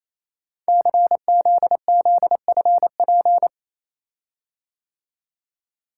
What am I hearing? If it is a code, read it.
Morse code, heard as a keyed tone, about 28 wpm, 700 Hz. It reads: CZZFP
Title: Code Groups - 5 Letter:Numbers - 28wpm